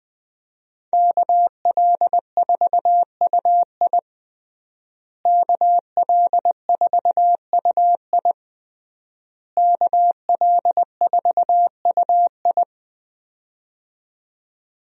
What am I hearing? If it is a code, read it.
Morse code, heard as a keyed tone, about 20 wpm, 700 Hz. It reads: KL4UI KL4UI KL4UI
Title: USA Call Signs - 5 Characters - Repeated 3x - 20wpm